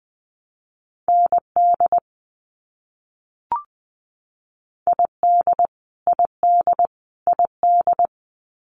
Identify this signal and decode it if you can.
Morse code, heard as a keyed tone, about 20 wpm, 700 Hz. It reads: ND ID ID ID
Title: US States Repeated 3x 20wpm